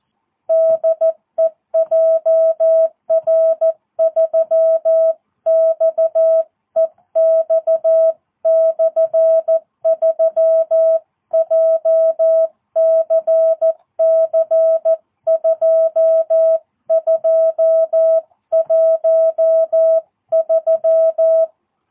Morse code, heard as a keyed tone, about 14 wpm, 650 Hz. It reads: DE JR3XEX/3 JCC2213
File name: jh8cbh / 2022cw